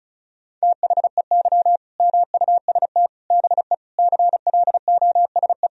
Morse code, heard as a keyed tone, about 36 wpm, 700 Hz. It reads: THEY MUST BE CLOSE
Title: Sentences from Top 1000 Words - Review Part 1 - 35wpm